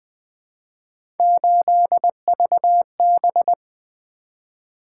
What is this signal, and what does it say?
Morse code, heard as a keyed tone, about 20 wpm, 700 Hz. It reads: 8VB
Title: Three Letter-Number Combinations - 20wpm